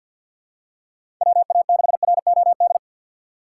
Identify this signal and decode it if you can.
Morse code, heard as a keyed tone, about 51 wpm, 700 Hz. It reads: WA6ROD
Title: USA Call Signs - 2x Character Spacing - 50wpm